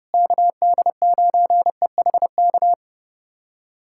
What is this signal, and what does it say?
Morse code, heard as a keyed tone, about 32 wpm, 700 Hz. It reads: KD9EHK
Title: USA Callsign DE Callsign - 6 Characters - 30wpm